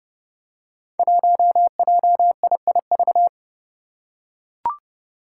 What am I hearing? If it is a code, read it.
Morse code, heard as a keyed tone, about 30 wpm, 700 Hz. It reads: 1JIIV
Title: Five Challenging Letter-Number Combinations 30wpm